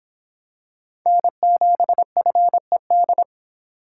Morse code, heard as a keyed tone, about 25 wpm, 700 Hz. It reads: N7FED